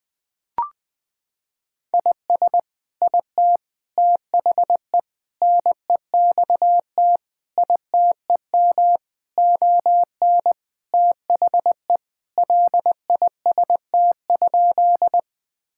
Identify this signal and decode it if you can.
Morse code, heard as a keyed tone, about 20 wpm, 700 Hz. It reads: IS IT THE NEXT ITEM ON THE LIST?
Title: Sentences from Top 1000 Words - Review Part 6 - 20wpm